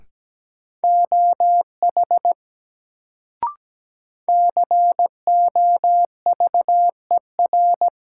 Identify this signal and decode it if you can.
Morse code, heard as a keyed tone, about 17 wpm, 700 Hz. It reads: OH COVER
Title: Sentences from Top 500 Words - New Word Drill - 17wpm